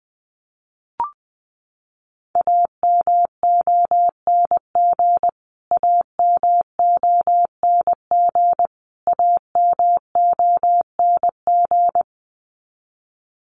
Morse code, heard as a keyed tone, about 20 wpm, 700 Hz. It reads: AMONG AMONG AMONG
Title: Top 500 Words Repeated 20wpm